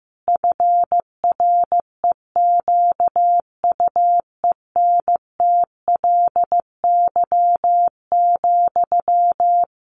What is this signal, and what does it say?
Morse code, heard as a keyed tone, about 15 wpm, 700 Hz. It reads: FREQUENTLY,
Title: Sentences from Top 1300 Words 15wpm